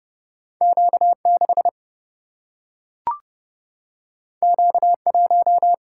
Morse code, heard as a keyed tone, about 30 wpm, 700 Hz. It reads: Q 6 Q 1
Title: Single-Letter Single-Number 30wpm